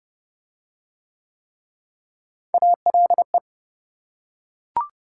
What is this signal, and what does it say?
Morse code, heard as a keyed tone, about 28 wpm, 700 Hz. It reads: ALE